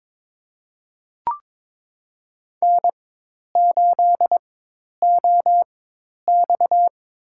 Morse code, heard as a keyed tone, about 22 wpm, spaced as for 10 wpm, 700 Hz. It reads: N8OX
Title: USA Call Signs - 4 Characters - 4x Character Spacing - 22wpm